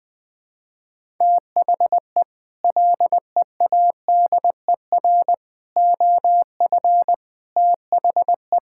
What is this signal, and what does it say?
Morse code, heard as a keyed tone, about 20 wpm, 700 Hz. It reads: THE LEADER OF THE
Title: Sets of 6 Words - 20wpm